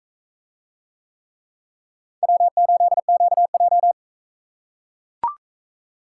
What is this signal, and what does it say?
Morse code, heard as a keyed tone, about 41 wpm, 700 Hz. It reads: W8QJ